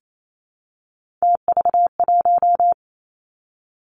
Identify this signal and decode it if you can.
Morse code, heard as a keyed tone, about 28 wpm, 700 Hz. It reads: TV1
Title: Three Challenging Letter-Number Combinations 28wpm